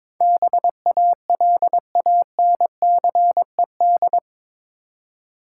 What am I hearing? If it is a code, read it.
Morse code, heard as a keyed tone, about 22 wpm, 700 Hz. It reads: BALANCED
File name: practice / 22wpm / Eight Letter Words - 22wpm